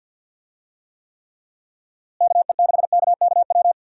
Morse code, heard as a keyed tone, about 46 wpm, 700 Hz. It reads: KE6KKW
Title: USA Call Signs - 6 Characters - Repeated 3x - 50wpm